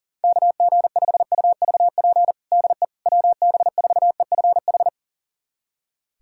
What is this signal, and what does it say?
Morse code, heard as a keyed tone, about 40 wpm, 700 Hz. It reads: KG5UVP DE WB4EFH
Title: USA Callsign DE Callsign 40wpm